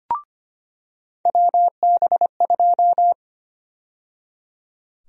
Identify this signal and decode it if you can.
Morse code, heard as a keyed tone, about 25 wpm, 700 Hz. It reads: WB2
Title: USA Call Sign Prefixes - 25wpm